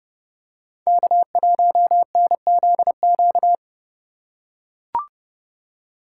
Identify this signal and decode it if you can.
Morse code, heard as a keyed tone, about 30 wpm, 700 Hz. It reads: K1NZQ
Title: USA Call Signs - 5 Characters - 4x Character Spacing - 30wpm